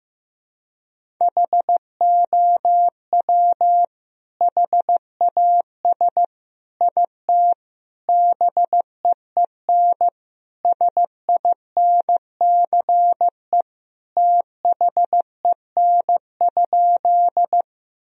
Morse code, heard as a keyed tone, about 16 wpm, 700 Hz. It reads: HOW HAS IT BEEN SINCE THEN?
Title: Sentences from Top 1000 Words - Review Part 1 - 15wpm